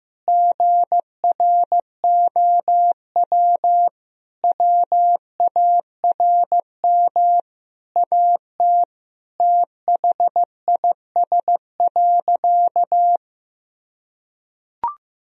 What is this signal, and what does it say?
Morse code, heard as a keyed tone, about 16 wpm, 700 Hz. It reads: GROW WARM AT THIS.
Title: Sentences from Top 1000 Words - Review Part 2 - 15wpm